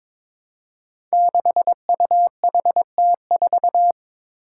Morse code, heard as a keyed tone, about 22 wpm, 700 Hz. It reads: 6UHT4
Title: Five Challenging Letter-Number Combinations 22wpm